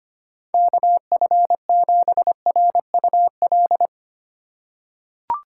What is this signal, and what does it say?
Morse code, heard as a keyed tone, about 25 wpm, 700 Hz. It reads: KF7RUL
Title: USA Call Signs - 6 Characters - 4x Character Spacing - 25wpm